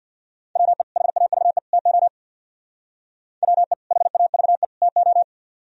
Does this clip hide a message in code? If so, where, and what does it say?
Yes: Morse code from 0.6 to 5.2 s, heard as a keyed tone, about 64 wpm, 700 Hz. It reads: WE HAVE TO WE HAVE TO